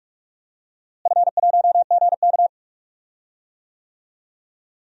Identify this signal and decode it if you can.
Morse code, heard as a keyed tone, about 46 wpm, 700 Hz. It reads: F1GK
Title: DXCC-100 Call Signs - 4 Characters - Repeated 3x - 45wpm